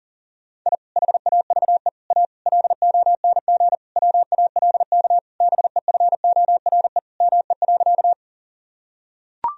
Morse code, heard as a keyed tone, about 40 wpm, 700 Hz. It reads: I HAVE A LONG WALK BEFORE ME.
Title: Sentences from Top 1000 Words - Review Part 1 - 40wpm